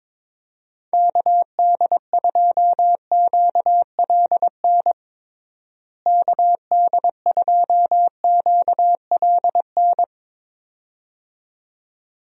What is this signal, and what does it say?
Morse code, heard as a keyed tone, about 22 wpm, 700 Hz. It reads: KD2QLN KD2QLN
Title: USA Call Signs - 6 Characters - Repeated 2x - 22wpm